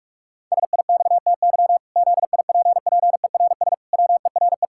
Morse code, heard as a keyed tone, about 46 wpm, 700 Hz. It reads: SIXTY ZIPPERS WERE
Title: Pangram Challenge v2 - 45wpm